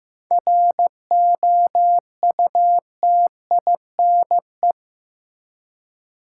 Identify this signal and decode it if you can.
Morse code, heard as a keyed tone, about 15 wpm, 700 Hz. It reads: ROUTINE
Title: Seven Letter Words - 15wpm